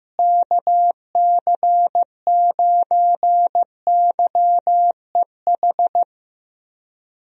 Morse code, heard as a keyed tone, about 15 wpm, 700 Hz. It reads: KC9YEH
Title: USA Call Signs - 6 Characters - Repeated 2x - 15wpm